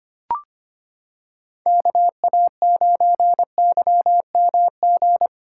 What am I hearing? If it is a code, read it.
Morse code, heard as a keyed tone, about 25 wpm, 700 Hz. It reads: KA9YMG
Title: K1USN SST QSO Practice - 25wpm